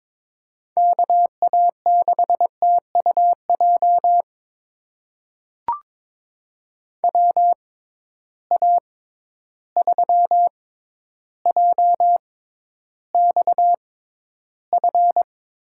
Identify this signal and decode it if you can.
Morse code, heard as a keyed tone, about 22 wpm, spaced as for 11 wpm, 700 Hz. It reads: KA6TUJ WA3JXF
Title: USA Call Signs - 6 Characters - 6x Character Spacing - 22wpm